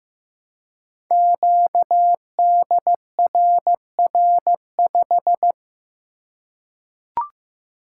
Morse code, heard as a keyed tone, about 15 wpm, 700 Hz. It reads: QDRR5
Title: Code Groups - 5 Letter:Numbers - 15wpm